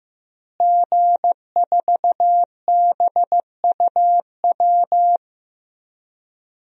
Morse code, heard as a keyed tone, about 16 wpm, 700 Hz. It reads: G4BUW
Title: DXCC-100 Call Signs - 5 Characters - Repeated 3x - 15wpm